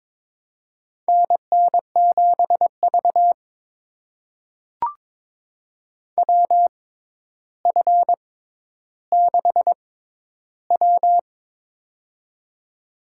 Morse code, heard as a keyed tone, about 22 wpm, spaced as for 11 wpm, 700 Hz. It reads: NN7V WF6W
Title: USA Call Signs - 4 Characters - 6x Character Spacing - 22wpm